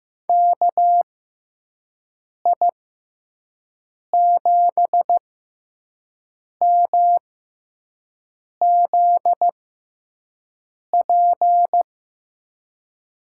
Morse code, heard as a keyed tone, about 15 wpm, spaced as for 5 wpm, 700 Hz. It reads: KI7MZP
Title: USA Call Signs - 6 Characters - 6x Character Spacing - 15wpm